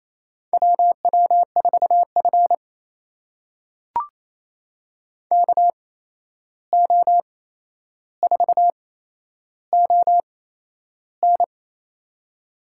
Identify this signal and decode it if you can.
Morse code, heard as a keyed tone, about 28 wpm, spaced as for 11 wpm, 700 Hz. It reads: WW4F KO4ON
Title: USA Call Signs - 8x Character Spacing - 28wpm